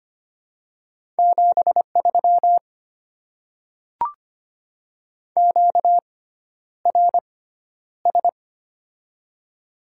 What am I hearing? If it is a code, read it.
Morse code, heard as a keyed tone, about 25 wpm, spaced as for 11 wpm, 700 Hz. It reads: 73 QRS
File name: Top 90 QSO Elements - 6x Character Spacing - 25wpm